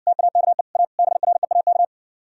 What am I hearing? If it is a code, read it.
Morse code, heard as a keyed tone, about 62 wpm, 700 Hz. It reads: TAKE A BREAK